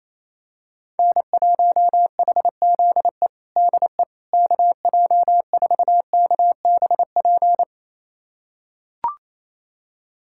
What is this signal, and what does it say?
Morse code, heard as a keyed tone, about 28 wpm, 700 Hz. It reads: N1HZE DE KJ4KBP
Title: USA Callsign DE Callsign 28wpm